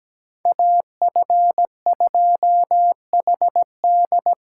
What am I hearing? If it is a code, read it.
Morse code, heard as a keyed tone, about 20 wpm, 700 Hz. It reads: AF2HD
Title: Speed Racing - USA 2x2 Call Signs 17wpm